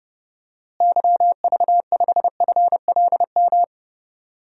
Morse code, heard as a keyed tone, about 30 wpm, 700 Hz. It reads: YV5FLM
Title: DXCC-100 Call Signs - 6 Characters - Repeated 3x - 30wpm